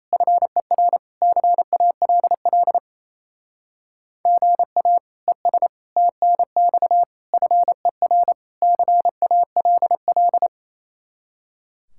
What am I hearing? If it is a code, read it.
Morse code, heard as a keyed tone, about 35 wpm, 700 Hz. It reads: FER CALL GA ES TNX FER CALL